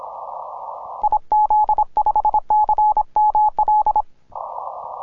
Morse code, heard as a keyed tone, about 26 wpm, 850 Hz. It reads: IZ5CML
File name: simple-file-list / tx5k 10mcw